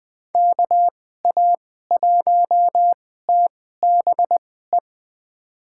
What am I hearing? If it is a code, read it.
Morse code, heard as a keyed tone, about 20 wpm, spaced as for 15 wpm, 700 Hz. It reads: KA1TBE
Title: USA Call Signs - 6 Characters - 2x Character Spacing - 20wpm